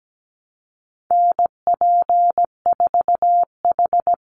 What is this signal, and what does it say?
Morse code, heard as a keyed tone, about 17 wpm, 700 Hz. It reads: NP4H
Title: Call Signs Repeated 3x - 17wpm